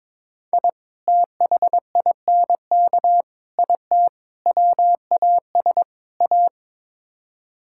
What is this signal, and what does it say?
Morse code, heard as a keyed tone, about 22 wpm, 700 Hz. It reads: I THINK IT WAS A
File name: practice / 22wpm / Build A Phrase - 5-Word Phrases - 22wpm